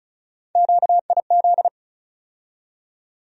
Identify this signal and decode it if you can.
Morse code, heard as a keyed tone, about 36 wpm, 700 Hz. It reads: QIZ
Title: USA Call Sign Postfixes - 3 Letters 35wpm